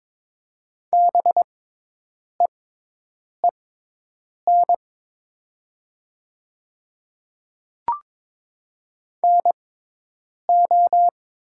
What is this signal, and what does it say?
Morse code, heard as a keyed tone, about 22 wpm, spaced as for 8 wpm, 700 Hz. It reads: BEEN NO
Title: 6x Character Spacing - Top 100 Words - 22wpm